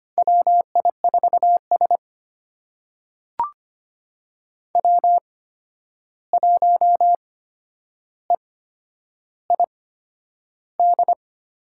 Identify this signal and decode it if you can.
Morse code, heard as a keyed tone, about 25 wpm, spaced as for 10 wpm, 700 Hz. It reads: WI4S W1EID
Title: USA Call Signs - 8x Character Spacing - 25wpm